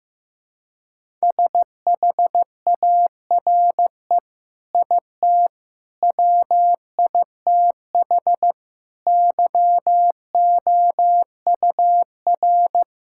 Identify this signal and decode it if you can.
Morse code, heard as a keyed tone, about 16 wpm, 700 Hz. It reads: SHARE IT WITH YOUR
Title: Sentences from Top 1000 Words - Review Part 6 - 15wpm